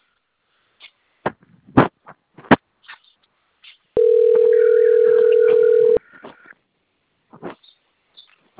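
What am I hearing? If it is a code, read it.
Ringback tone (North America), 440 + 480 Hz.
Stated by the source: Outbound telephone call placed through the Cornelio Procopio system.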